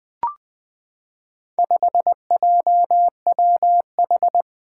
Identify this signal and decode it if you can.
Morse code, heard as a keyed tone, about 21 wpm, 700 Hz. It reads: 5JWH
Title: Four Challenging Letter-Number Combinations 20wpm